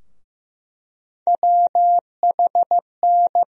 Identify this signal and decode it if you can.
Morse code, heard as a keyed tone, about 15 wpm, 700 Hz. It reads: WHN
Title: USA Call Sign Postfixes - 3 Letters 15wpm